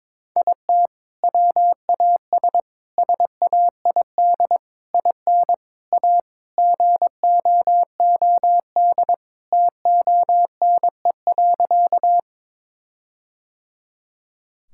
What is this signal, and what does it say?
Morse code, heard as a keyed tone, about 21 wpm, 700 Hz. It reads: IT WAS SAID IN A GOOD TONE.
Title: Sentences from Top 1600 Words 22wpm